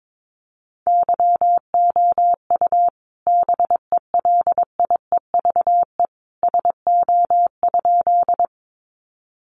Morse code, heard as a keyed tone, about 22 wpm, 700 Hz. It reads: YOU BELIEVE SO?
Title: Sentences from Top 1000 Words 22wpm